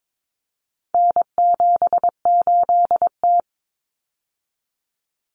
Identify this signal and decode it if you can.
Morse code, heard as a keyed tone, about 22 wpm, 700 Hz. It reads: N78T
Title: Common Cut Numbers - 4-Digit 22wpm